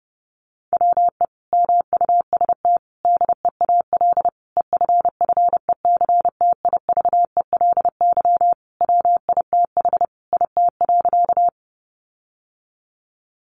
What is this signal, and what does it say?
Morse code, heard as a keyed tone, about 30 wpm, 700 Hz. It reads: WE MUST DEAL EFFECTIVELY WITH IT.
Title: Sentences from Top 1400 Words 30wpm